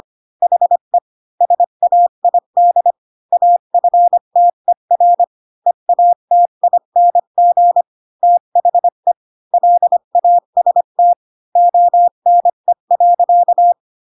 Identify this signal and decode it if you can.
Morse code, heard as a keyed tone, about 25 wpm, 700 Hz. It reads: HE SAID AFTER EATING THE LAST ONE.
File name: Pinocchio / Pinocchio ch7 20 25 ik1hge